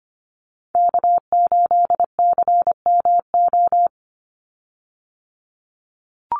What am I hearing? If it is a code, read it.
Morse code, heard as a keyed tone, about 24 wpm, 700 Hz. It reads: K8CMO
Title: Call Signs Repeated 3x - 25wpm